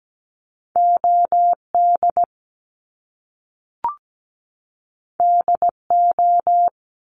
Morse code, heard as a keyed tone, about 17 wpm, 700 Hz. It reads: OD DO
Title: Top Two Letter Combinations in Words 17wpm